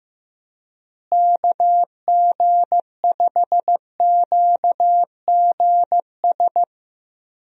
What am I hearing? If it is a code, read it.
Morse code, heard as a keyed tone, about 15 wpm, 700 Hz. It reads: KG5QGS
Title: USA Call Signs - 6 Characters - Repeated 3x - 15wpm